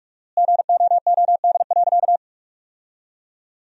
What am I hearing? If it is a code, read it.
Morse code, heard as a keyed tone, about 47 wpm, 700 Hz. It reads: GOOD.
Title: 4x Word-Spacing - Sentences from Top 200 Words - 45wpm